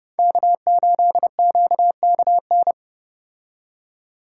Morse code, heard as a keyed tone, about 30 wpm, 700 Hz. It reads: K8QKN